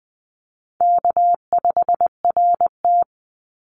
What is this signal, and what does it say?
Morse code, heard as a keyed tone, about 20 wpm, 700 Hz. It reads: K5RT
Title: Easy Call Signs - 20wpm